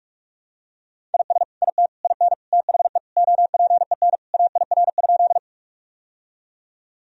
Morse code, heard as a keyed tone, about 44 wpm, 700 Hz. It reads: IS IT IN THE OPEN AIR?
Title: Sentences from Top 1000 Words - Review Part 1 - 45wpm